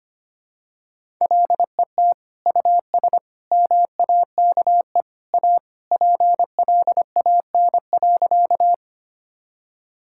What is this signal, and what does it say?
Morse code, heard as a keyed tone, about 25 wpm, 700 Hz. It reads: LET US MAKE A PLAN.
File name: practice / 25wpm / Sentences with 2-Word Phrases - 25wpm